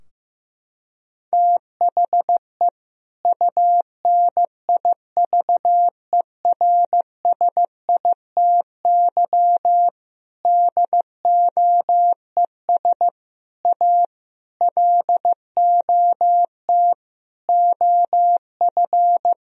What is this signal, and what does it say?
Morse code, heard as a keyed tone, about 15 wpm, 700 Hz. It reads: THE UNIVERSITY DOES A LOT OF
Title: Sentences from Top 1000 Words - Review Part 4 - 15wpm